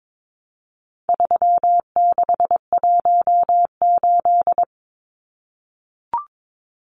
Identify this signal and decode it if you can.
Morse code, heard as a keyed tone, about 22 wpm, 700 Hz. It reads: 3618
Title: Numbers - 4-Digit 22wpm